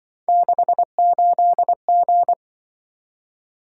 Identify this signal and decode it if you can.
Morse code, heard as a keyed tone, about 25 wpm, 700 Hz. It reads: 68G
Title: Speed-Racing - Three Letter-Number Combinations - 20wpm